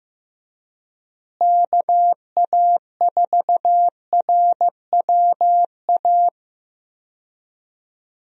Morse code, heard as a keyed tone, about 15 wpm, 700 Hz. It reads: KA4RWA
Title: USA Call Signs - 6 Characters - Repeated 2x - 15wpm